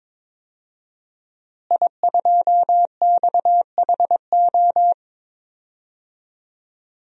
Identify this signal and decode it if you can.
Morse code, heard as a keyed tone, about 22 wpm, 700 Hz. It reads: I2XHO
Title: DXCC-100 Call Signs - 5 Characters - Repeated 3x - 22wpm